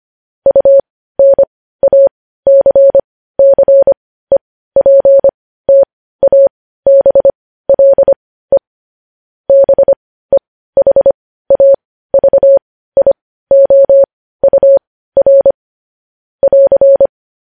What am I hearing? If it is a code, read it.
Morse code, heard as a keyed tone, about 25 wpm, spaced as for 15 wpm, 550 Hz. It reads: UNACCEPTABLE BEHAVIOUR <AR>
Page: News Headlines in Morse Code at 15 WPM